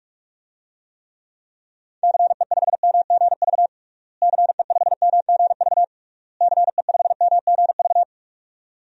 Morse code, heard as a keyed tone, about 46 wpm, 700 Hz. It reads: CE5MGV CE5MGV CE5MGV
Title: DXCC-100 Call Signs - 6 Characters - Repeated 3x - 45wpm